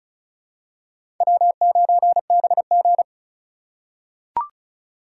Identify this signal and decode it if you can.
Morse code, heard as a keyed tone, about 34 wpm, 690 Hz. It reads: W9BG
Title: USA Call Signs - 3x Character Spacing - 35wpm